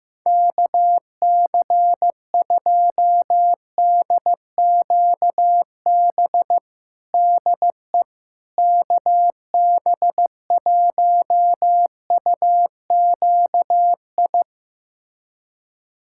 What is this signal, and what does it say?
Morse code, heard as a keyed tone, about 15 wpm, 700 Hz. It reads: KC2DQB DE KB1UQI
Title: USA Callsign DE Callsign 15wpm